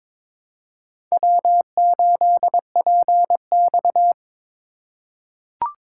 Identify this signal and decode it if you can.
Morse code, heard as a keyed tone, about 22 wpm, 700 Hz. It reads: W8PX
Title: USA Call Signs - 4 Characters - 2x Character Spacing - 22wpm